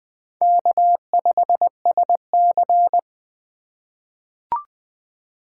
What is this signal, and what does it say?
Morse code, heard as a keyed tone, about 20 wpm, 700 Hz. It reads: K5SC
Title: USA Call Signs - 4 Characters - 2x Character Spacing - 20wpm